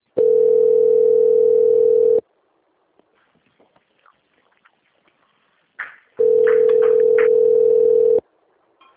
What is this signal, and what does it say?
Ringback tone (North America), 440 + 480 Hz.